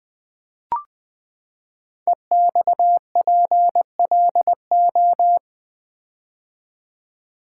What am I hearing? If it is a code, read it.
Morse code, heard as a keyed tone, about 20 wpm, 700 Hz. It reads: EXPLO
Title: Build A Word - 5 Letters At A Time - 12-Letter Words - 20wpm